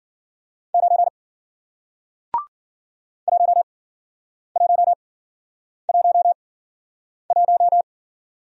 Morse code, heard as a keyed tone, about 53 wpm, spaced as for 9 wpm, 700 Hz. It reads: 9 1111